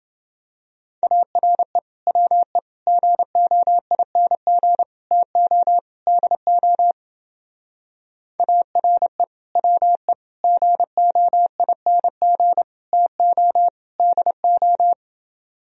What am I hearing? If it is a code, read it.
Morse code, heard as a keyed tone, about 29 wpm, 700 Hz. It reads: ARE WE GOING TO DO ARE WE GOING TO DO